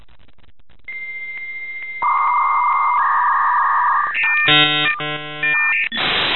Dial-up modem handshake.
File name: modem / modem28a